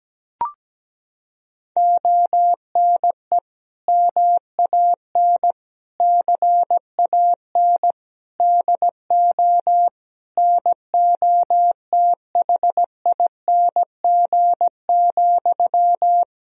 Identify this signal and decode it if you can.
Morse code, heard as a keyed tone, about 17 wpm, 700 Hz. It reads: ONE MAN CAN DO NOTHING,
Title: Sentences from Top 1000 Words - Review Part 4 - 17wpm